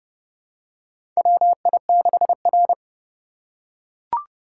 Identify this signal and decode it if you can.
Morse code, heard as a keyed tone, about 30 wpm, 700 Hz. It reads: WI6R
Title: USA Call Signs - 4 Characters - 3x Character Spacing - 30wpm